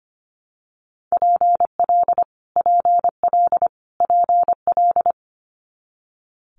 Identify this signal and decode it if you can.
Morse code, heard as a keyed tone, about 25 wpm, 700 Hz. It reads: PL PL PL
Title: Multi-letter Phonemes Repeated 3x 25wpm